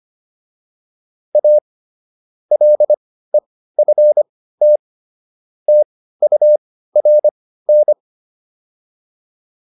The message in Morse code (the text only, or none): A LEFT TURN